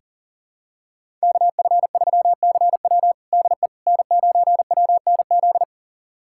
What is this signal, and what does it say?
Morse code, heard as a keyed tone, about 40 wpm, 700 Hz. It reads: KF3CW DE N9WNZ
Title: USA Callsign DE Callsign - 5 Characters - 40wpm